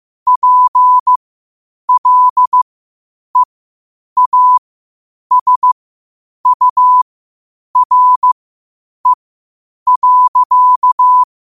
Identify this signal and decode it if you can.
Morse code, heard as a keyed tone, about 15 wpm, spaced as for 8 wpm, 1000 Hz. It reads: PLEASURE.
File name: 8wpm